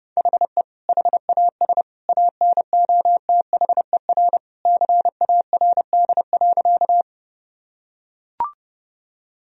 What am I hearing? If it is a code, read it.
Morse code, heard as a keyed tone, about 32 wpm, 700 Hz. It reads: HE HAS ANOTHER CARD.
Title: Sentences from Top 1000 Words - Review Part 4 - 30wpm